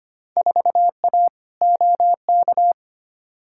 Morse code, heard as a keyed tone, about 25 wpm, 700 Hz. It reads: 4A OK
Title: Field Day Exchanges - 25wpm